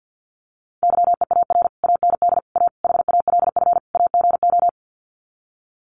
Morse code, heard as a keyed tone, about 50 wpm, 700 Hz. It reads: YEAR AND A HALF AGO
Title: Sets of 5 Words Encore 50wpm